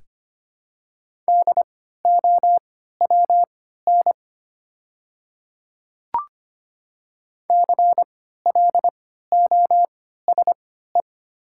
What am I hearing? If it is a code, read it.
Morse code, heard as a keyed tone, about 25 wpm, spaced as for 14 wpm, 700 Hz. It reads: DOWN CLOSE
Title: 3x Character Spacing - Top 500 Words - 25wpm